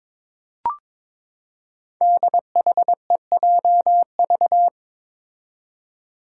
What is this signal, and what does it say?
Morse code, heard as a keyed tone, about 22 wpm, 700 Hz. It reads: DHEJV
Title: Five Challenging Letter-Number Combinations 22wpm